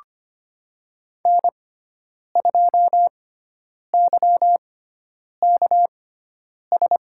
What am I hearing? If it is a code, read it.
Morse code, heard as a keyed tone, about 25 wpm, spaced as for 9 wpm, 700 Hz. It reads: N2YKS
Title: USA Call Signs - 5 Characters - 6x Character Spacing - 25wpm